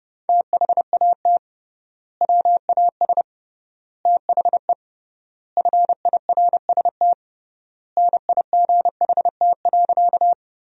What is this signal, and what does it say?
Morse code, heard as a keyed tone, about 30 wpm, 700 Hz. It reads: THAT WAS THE FIRST NIGHT.
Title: 3x Word-Spacing - Sentences from Top 300 Words - 30wpm